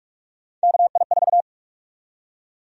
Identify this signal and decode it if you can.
Morse code, heard as a keyed tone, about 46 wpm, 700 Hz. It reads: KI4
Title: USA Call Sign Prefixes - 45wpm